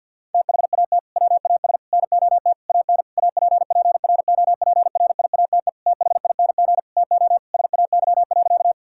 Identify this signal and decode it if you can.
Morse code, heard as a keyed tone, about 50 wpm, 700 Hz. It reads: THAT WAS NOT AN APPROPRIATE THING TO SAY.
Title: Sentences from Top 1000 Words - Review Part 5 - 50wpm